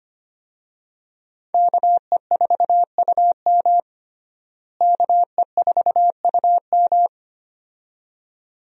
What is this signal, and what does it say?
Morse code, heard as a keyed tone, about 25 wpm, 700 Hz. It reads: KE4UM KE4UM
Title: USA Call Signs - 5 Characters - Repeated 2x - 25wpm